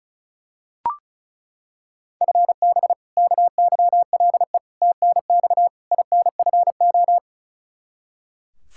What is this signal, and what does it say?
Morse code, heard as a keyed tone, about 36 wpm, 700 Hz. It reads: FB KYLE TNX INFO
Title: Practice QSO - Exchange 3 - More Info - 35wpm